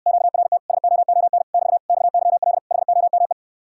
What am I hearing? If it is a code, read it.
Morse code, heard as a keyed tone, about 69 wpm, 700 Hz. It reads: QRT SOON <BT> XYL HOME